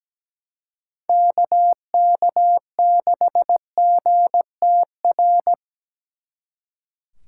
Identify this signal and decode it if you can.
Morse code, heard as a keyed tone, about 18 wpm, 700 Hz. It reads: KK6GTR
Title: Speed Racing - USA Call Signs 17wpm